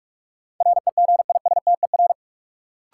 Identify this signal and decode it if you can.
Morse code, heard as a keyed tone, about 42 wpm, 700 Hz. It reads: REGISTER
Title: Sentences from Top 1200 Words - New Word Drill Alt - 45wpm